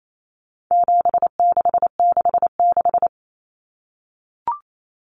Morse code, heard as a keyed tone, about 28 wpm, 700 Hz. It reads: 7666
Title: Numbers - 4-Digit 28wpm